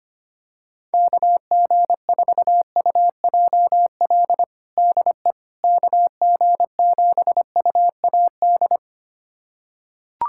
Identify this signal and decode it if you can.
Morse code, heard as a keyed tone, about 25 wpm, 700 Hz. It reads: KG4UJL DE KG7UAD
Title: USA Callsign DE Callsign 25wpm